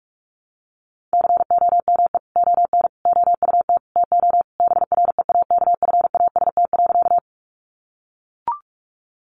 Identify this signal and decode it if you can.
Morse code, heard as a keyed tone, about 44 wpm, 700 Hz. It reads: COME ON OUT TO BREAKFAST.
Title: Sentences from Top 1500 Words 45wpm